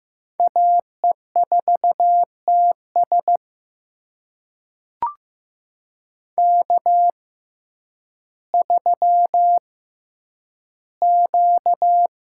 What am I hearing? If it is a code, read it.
Morse code, heard as a keyed tone, about 15 wpm, spaced as for 9 wpm, 700 Hz. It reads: AE4TS K3Q
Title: USA Call Signs - 5 Characters - 6x Character Spacing - 15wpm